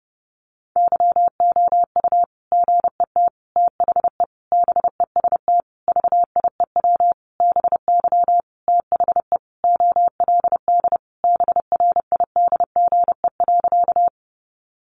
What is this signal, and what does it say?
Morse code, heard as a keyed tone, about 30 wpm, 700 Hz. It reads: YOU GET THE BEST VIEW BY THE OLD BRIDGE.
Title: Sentences from Top 1200 Words 30wpm